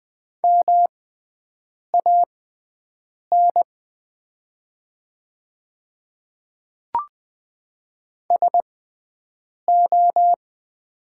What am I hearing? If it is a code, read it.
Morse code, heard as a keyed tone, about 20 wpm, spaced as for 7 wpm, 700 Hz. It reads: MAN SO
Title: 6x Character Spacing - Top 200 Words - 20wpm